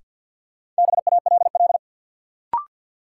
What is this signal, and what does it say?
Morse code, heard as a keyed tone, about 51 wpm, 700 Hz. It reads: BALL
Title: Sentences from Top 900 Words - New Word Drill - 50wpm